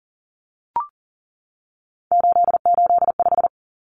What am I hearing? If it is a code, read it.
Morse code, heard as a keyed tone, about 40 wpm, 700 Hz. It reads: 885
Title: Numbers - 3-Digit 40wpm